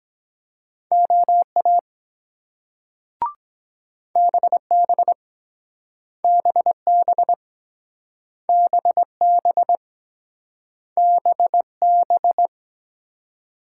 Morse code, heard as a keyed tone, about 22 wpm, 700 Hz. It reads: OA BB BB BB BB